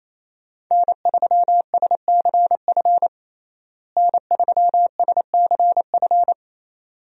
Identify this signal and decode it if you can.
Morse code, heard as a keyed tone, about 28 wpm, 700 Hz. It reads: N3SCF N3SCF
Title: USA Call Signs - 5 Characters - Repeated 2x - 28wpm